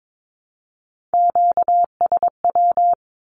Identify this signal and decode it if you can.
Morse code, heard as a keyed tone, about 22 wpm, 700 Hz. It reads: QSW
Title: Q-Codes 22wpm